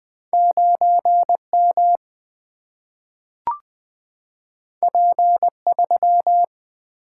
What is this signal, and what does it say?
Morse code, heard as a keyed tone, about 20 wpm, 700 Hz. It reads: 9 M P 3
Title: Single-Letter Single-Number 20wpm